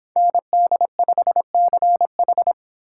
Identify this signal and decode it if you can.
Morse code, heard as a keyed tone, about 30 wpm, 700 Hz. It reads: ND5CH